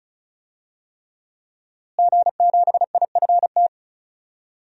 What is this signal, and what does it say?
Morse code, heard as a keyed tone, about 36 wpm, 700 Hz. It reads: G7IFT
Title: DXCC-100 Call Signs - 5 Characters - Repeated 3x - 35wpm